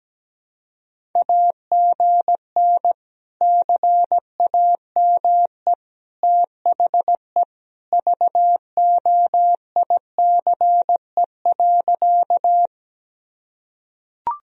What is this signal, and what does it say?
Morse code, heard as a keyed tone, about 17 wpm, 700 Hz. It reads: AGN CAME THE VOICE.
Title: Sentences from Top 500 Words and Abbreviations 17wpm